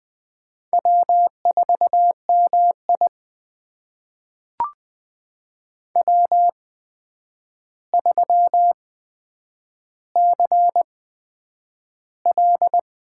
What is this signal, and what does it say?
Morse code, heard as a keyed tone, about 20 wpm, spaced as for 9 wpm, 700 Hz. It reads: W4MI W3CL
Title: USA Call Signs - 4 Characters - 8x Character Spacing - 20wpm